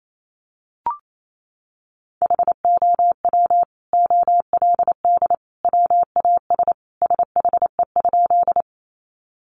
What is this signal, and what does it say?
Morse code, heard as a keyed tone, about 28 wpm, 700 Hz. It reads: HOW OLD WAS SHE?
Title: Sentences from Top 200 Words 28wpm